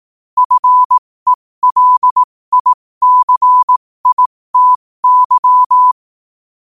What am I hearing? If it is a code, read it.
Morse code, heard as a keyed tone, about 18 wpm, spaced as for 15 wpm, 1000 Hz. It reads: FELICITY
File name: qotdfiles / 15wpm